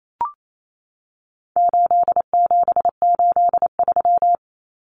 Morse code, heard as a keyed tone, about 28 wpm, 700 Hz. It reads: 8783
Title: Numbers - 4-Digit 28wpm